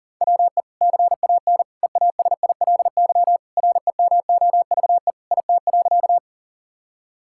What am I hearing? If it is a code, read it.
Morse code, heard as a keyed tone, about 40 wpm, 700 Hz. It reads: WE CAN EASILY REMOVE IT.
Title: Sentences from Top 1000 Words - Review Part 5 - 40wpm